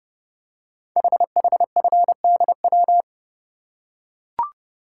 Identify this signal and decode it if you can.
Morse code, heard as a keyed tone, about 30 wpm, 700 Hz. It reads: HHFDW
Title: Code Groups - 5 Letters - 30wpm